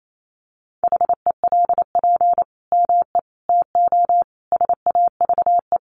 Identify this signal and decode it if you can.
Morse code, heard as a keyed tone, about 28 wpm, 700 Hz. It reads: HELP ME TO SAVE
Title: Sentences from Top 1100 Words 28wpm